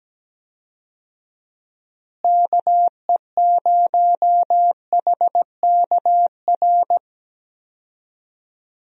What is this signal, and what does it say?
Morse code, heard as a keyed tone, about 17 wpm, 700 Hz. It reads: KE0HKR
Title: USA Call Signs - 6 Characters - Repeated 3x - 17wpm